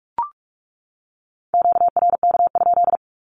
Morse code, heard as a keyed tone, about 44 wpm, 700 Hz. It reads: QRK?
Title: Q-Codes 45wpm